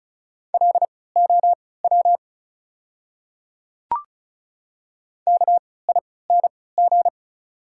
Morse code, heard as a keyed tone, about 36 wpm, spaced as for 20 wpm, 690 Hz. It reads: LOW KING
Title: 3x Character Spacing - Top 500 Words - 35wpm